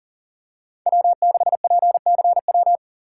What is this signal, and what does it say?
Morse code, heard as a keyed tone, about 40 wpm, 700 Hz. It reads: W6PCW
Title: USA Call Signs - 5 Characters - 4x Character Spacing - 40wpm